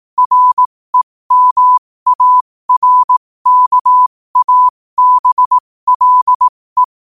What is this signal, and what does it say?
Morse code, heard as a keyed tone, about 18 wpm, spaced as for 15 wpm, 1000 Hz. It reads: REMARKABLE
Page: New quotes every day in morse code at 15 Words per minute.